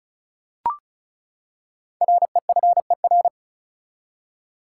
Morse code, heard as a keyed tone, about 36 wpm, 700 Hz. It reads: REFER
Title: Five Letter Words - 35wpm